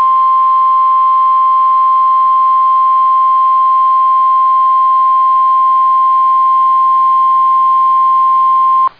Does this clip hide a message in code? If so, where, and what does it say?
Yes: Emergency Alert System (SAME) tones (the 1050 Hz attention tone) from 0.8 to 8.2 s.